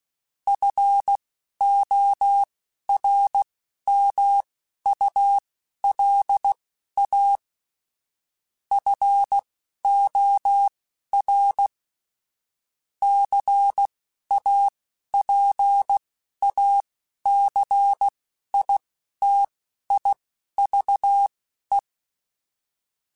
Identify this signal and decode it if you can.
Morse code, heard as a keyed tone, about 16 wpm, spaced as for 11 wpm, 800 Hz. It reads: FORMULA FOR CAPACITIVE
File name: Technical Subjects / Technical Subjects-mp3s / XC @10